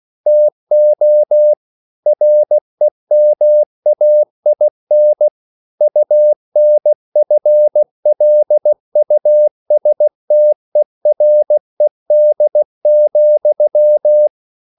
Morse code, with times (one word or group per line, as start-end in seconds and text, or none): 0.3-1.5 TO
2.1-5.3 REMAIN
5.8-14.3 UNFLUSTERED,